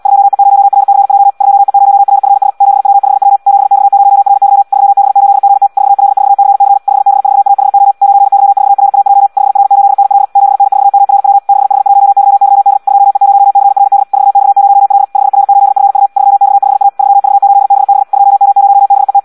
Morse code, heard as a keyed tone, about 78 wpm, 800 Hz. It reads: 2E0RCJ 2E0RFS 7W5W 7X0RY 3B8GT 4L4CC 5B4AHO 9Y4RAJ 5R8NL 7N4MRW 6H1YYD 2E0CRD 4X0L 5R8FU 3X5A 3V8BB 2M0BET
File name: Audio practica CW / qrq-tx-78 2